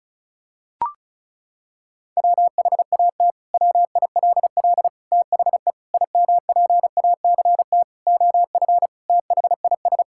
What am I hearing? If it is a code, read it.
Morse code, heard as a keyed tone, about 36 wpm, 700 Hz. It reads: WHAT WILL THE IMPACT OF THIS
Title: Sentences from Top 1000 Words - Review Part 6 - 35wpm